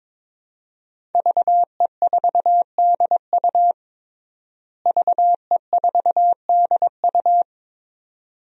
Morse code, heard as a keyed tone, about 21 wpm, 700 Hz. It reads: VE4DU VE4DU
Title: DXCC-100 Call Signs - 5 Characters - Repeated 2x - 22wpm